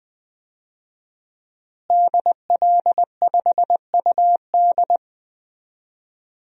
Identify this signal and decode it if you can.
Morse code, heard as a keyed tone, about 20 wpm, 700 Hz. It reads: DL5UD
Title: DXCC-100 Call Signs - 5 Characters - Repeated 2x - 20wpm